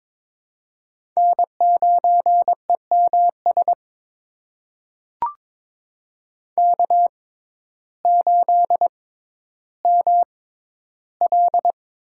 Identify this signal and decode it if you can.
Morse code, heard as a keyed tone, about 22 wpm, spaced as for 12 wpm, 700 Hz. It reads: N9EMS K8ML
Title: USA Call Signs - 5 Characters - 6x Character Spacing - 22wpm